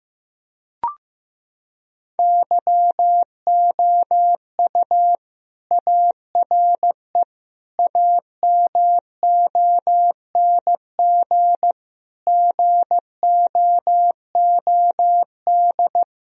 Morse code, heard as a keyed tone, about 16 wpm, 700 Hz. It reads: YOU ARE AMONG GOOD
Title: Sentences from Top 1000 Words - Review Part 3 - 15wpm